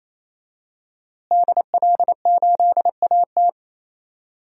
Morse code, heard as a keyed tone, about 28 wpm, 700 Hz. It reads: DL8AT
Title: DXCC-100 Call Signs - 5 Characters - Repeated 3x - 28wpm